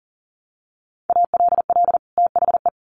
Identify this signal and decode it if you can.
Morse code, heard as a keyed tone, about 40 wpm, 700 Hz. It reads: ALL THE
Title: Sets of 2 Words 40wpm